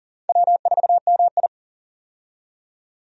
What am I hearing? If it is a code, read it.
Morse code, heard as a keyed tone, about 40 wpm, 700 Hz. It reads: W4MI
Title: USA Call Signs - 4 Characters - 40wpm